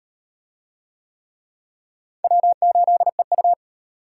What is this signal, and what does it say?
Morse code, heard as a keyed tone, about 34 wpm, 700 Hz. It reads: W8EU